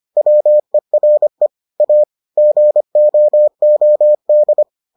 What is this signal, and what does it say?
Morse code, heard as a keyed tone, about 25 wpm, 600 Hz. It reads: WERE A GOOD